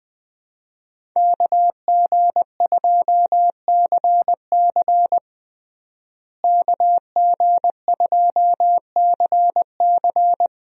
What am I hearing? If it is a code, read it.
Morse code, heard as a keyed tone, about 21 wpm, 700 Hz. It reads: KG2CC KG2CC
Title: USA Call Signs - 5 Characters - Repeated 2x - 20wpm